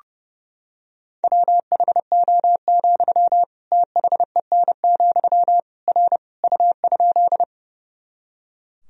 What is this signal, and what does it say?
Morse code, heard as a keyed tone, about 32 wpm, 700 Hz. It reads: WHO, THEN, R U?
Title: Sentences from Top 100 Words and Abbreviations 30wpm